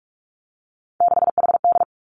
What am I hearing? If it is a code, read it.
Morse code, heard as a keyed tone, about 46 wpm, 700 Hz. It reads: 6HD
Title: Three Challenging Letter-Number Combinations 45wpm